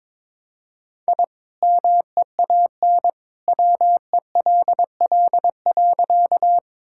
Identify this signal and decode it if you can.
Morse code, heard as a keyed tone, about 23 wpm, 700 Hz. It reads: I MEAN WELL.
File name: practice / 22wpm / Sentences with 2-Word Phrases - 22wpm